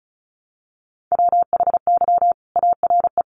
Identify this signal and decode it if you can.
Morse code, heard as a keyed tone, about 34 wpm, 700 Hz. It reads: WHY ARE
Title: Sentences from Top 200 Words 35wpm